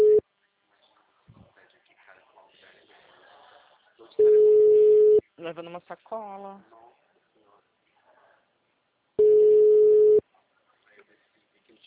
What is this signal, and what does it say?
Ringback tone (Europe, 425 Hz), with speech.